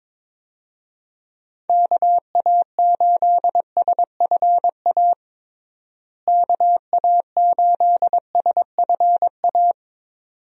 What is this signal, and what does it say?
Morse code, heard as a keyed tone, about 21 wpm, 700 Hz. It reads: KA8SFA KA8SFA
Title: USA Call Signs - 6 Characters - Repeated 2x - 22wpm